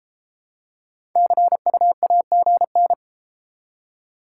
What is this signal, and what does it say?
Morse code, heard as a keyed tone, about 31 wpm, 700 Hz. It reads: CU AGN